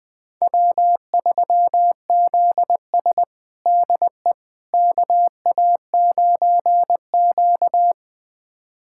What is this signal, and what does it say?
Morse code, heard as a keyed tone, about 21 wpm, 700 Hz. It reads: W3ZS DE KA9Q
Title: USA Callsign DE Callsign - 4 Characters - 20wpm